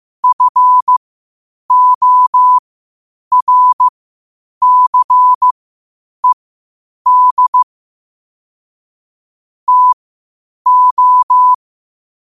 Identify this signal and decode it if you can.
Morse code, heard as a keyed tone, about 15 wpm, spaced as for 8 wpm, 1000 Hz. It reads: FORCED TO
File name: qotdfiles / 8wpm